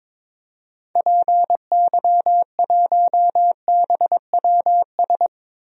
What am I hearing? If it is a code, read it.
Morse code, heard as a keyed tone, about 22 wpm, 700 Hz. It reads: PY1BWS
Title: DXCC-100 Call Signs - 6 Characters - 22wpm